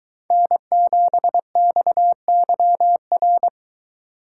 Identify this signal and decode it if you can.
Morse code, heard as a keyed tone, about 23 wpm, 700 Hz. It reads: N7XYR